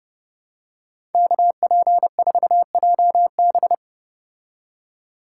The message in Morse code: KP4JB

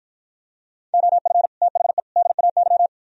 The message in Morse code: OF THE DAY

